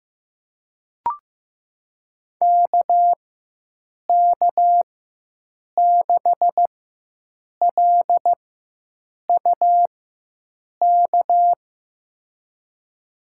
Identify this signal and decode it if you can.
Morse code, heard as a keyed tone, about 15 wpm, spaced as for 7 wpm, 700 Hz. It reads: KK6LUK
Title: USA Call Signs - 6 Characters - 4x Character Spacing - 15wpm